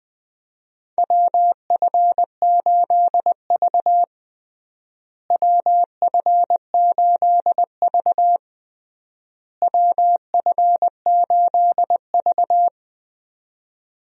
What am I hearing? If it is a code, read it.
Morse code, heard as a keyed tone, about 20 wpm, 700 Hz. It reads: WF8V WF8V WF8V
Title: USA Call Signs - 4 Characters - Repeated 3x - 20wpm